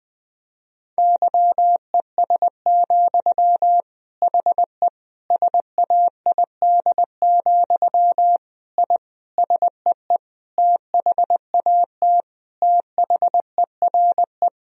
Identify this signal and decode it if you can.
Morse code, heard as a keyed tone, about 20 wpm, 700 Hz. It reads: YES, HE SAID, I SEE THAT THERE
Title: Sentences from Top 1000 Words - Review Part 1 - 20wpm